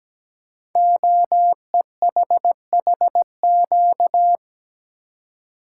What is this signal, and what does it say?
Morse code, heard as a keyed tone, about 17 wpm, 700 Hz. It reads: OEHHQ
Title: Code Groups - 5 Letters - 17wpm